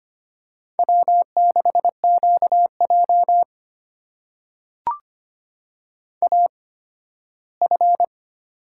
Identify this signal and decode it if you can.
Morse code, heard as a keyed tone, about 25 wpm, spaced as for 15 wpm, 700 Hz. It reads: W6QJ AF
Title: USA Call Signs - 4 Characters - 8x Character Spacing - 25wpm